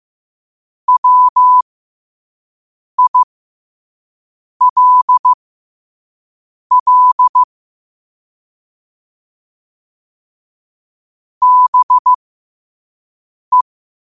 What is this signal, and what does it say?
Morse code, heard as a keyed tone, about 15 wpm, spaced as for 5 wpm, 1000 Hz. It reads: WILL BE